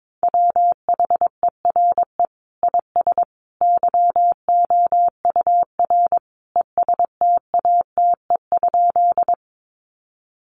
Morse code, heard as a keyed tone, about 22 wpm, 700 Hz. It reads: WHERE IS YOUR ESTATE?
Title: Sentences from Top 1200 Words 22wpm